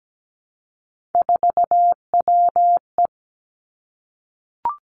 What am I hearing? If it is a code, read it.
Morse code, heard as a keyed tone, about 17 wpm, 700 Hz. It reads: 4WE
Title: Three Challenging Letter-Number Combinations 17wpm